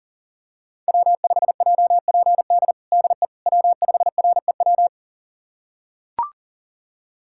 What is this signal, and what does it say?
Morse code, heard as a keyed tone, about 40 wpm, 700 Hz. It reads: W5JPD DE W5REW
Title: USA Callsign DE Callsign - 5 Characters - 40wpm